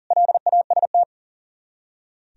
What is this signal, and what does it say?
Morse code, heard as a keyed tone, about 52 wpm, 700 Hz. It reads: LAST